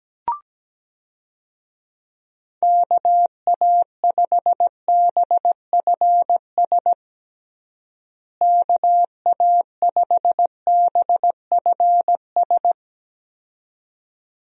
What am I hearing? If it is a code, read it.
Morse code, heard as a keyed tone, about 17 wpm, 700 Hz. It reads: KA5BFS KA5BFS
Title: USA Call Signs - 6 Characters - Repeated 2x - 17wpm